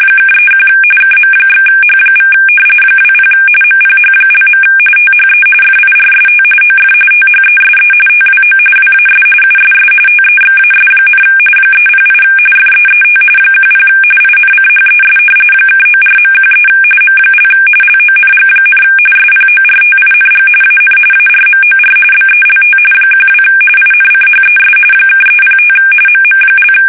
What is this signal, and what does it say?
RTTY, about 45.45 baud. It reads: (31 KM/H= 20 MPH). THE FIRST OF THESE TWO WAS USS TEXAS, 308 FEET 10 INCHES (94.13 M) LONG, SPORTING AN ARMOR BELT 12 INCHES (305 MM)
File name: fsk850 / battleships